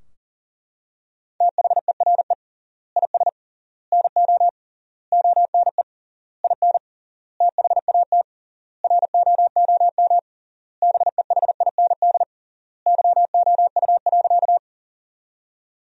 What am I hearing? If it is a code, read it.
Morse code, heard as a keyed tone, about 40 wpm, 700 Hz. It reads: THERE IS NO ONE IN THAT ROOM BEHIND YOU.
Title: 3x Word-Spacing - Sentences from Top 500 Words - 40wpm